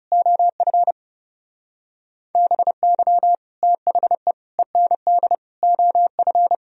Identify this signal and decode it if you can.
Morse code, heard as a keyed tone, about 37 wpm, 700 Hz. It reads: OF BY THE END OF
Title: Speed-Racing - Sets of 4 Words 30wpm